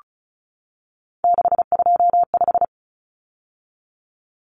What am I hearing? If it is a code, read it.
Morse code, heard as a keyed tone, about 34 wpm, 700 Hz. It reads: 625
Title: Numbers - 3-Digit 35wpm